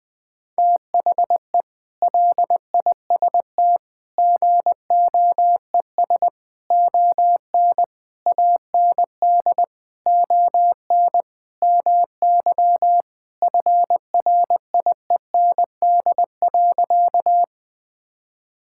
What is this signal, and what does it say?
Morse code, heard as a keyed tone, about 20 wpm, 700 Hz. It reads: THE LIST GOES ON AND ON MY FRIEND.
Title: Sentences with 6-Word Phrases - Spoken Beginning - 20wpm